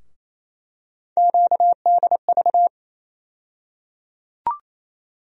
Morse code, heard as a keyed tone, about 27 wpm, 700 Hz. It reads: QDV
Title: USA Call Sign Postfixes - 3 Letters 28wpm